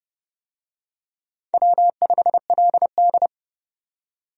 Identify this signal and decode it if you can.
Morse code, heard as a keyed tone, about 29 wpm, 700 Hz. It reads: W5LD
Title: USA Call Signs - 4 Characters - 30wpm